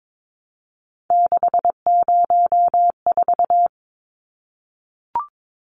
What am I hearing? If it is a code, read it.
Morse code, heard as a keyed tone, about 22 wpm, 700 Hz. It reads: 604
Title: Numbers - 3-Digit 22wpm